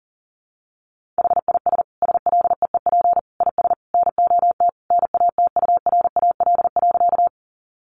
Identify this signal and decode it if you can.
Morse code, heard as a keyed tone, about 40 wpm, 700 Hz. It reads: HIS SLEEP IS NOT NATURAL.
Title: Sentences from Top 700 Words 40wpm